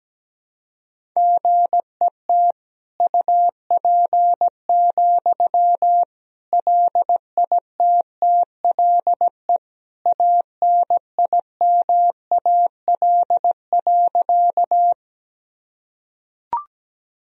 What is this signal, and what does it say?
Morse code, heard as a keyed tone, about 17 wpm, 700 Hz. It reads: GET UP, LITTLE ANIMAL.
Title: Sentences from Top 1000 Words - Review Part 1 - 17wpm